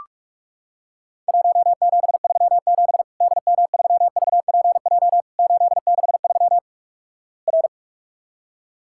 Morse code, heard as a keyed tone, about 46 wpm, 700 Hz. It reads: 1737 DM3VPJ 863 R
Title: Worked All Europe DX Contest - QTC Exchange - 45wpm